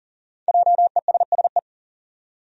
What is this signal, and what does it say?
Morse code, heard as a keyed tone, about 40 wpm, 700 Hz. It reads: JESSE
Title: USA Common Names - Max 5 Letters - 40wpm